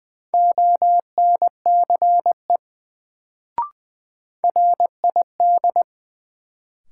Morse code, heard as a keyed tone, about 20 wpm, 700 Hz. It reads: ONCE RID
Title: Sets of 4 Words - New Word Drill - 20wpm